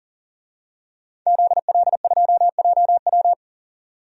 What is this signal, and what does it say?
Morse code, heard as a keyed tone, about 38 wpm, 700 Hz. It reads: ZL2JW
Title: DXCC-100 Call Signs - 5 Characters - 40wpm